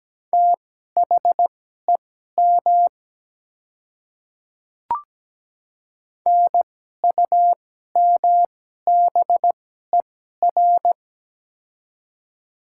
Morse code, heard as a keyed tone, about 17 wpm, spaced as for 12 wpm, 690 Hz. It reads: THEM NUMBER
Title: 2x Character Spacing - Top 100 Words - 17wpm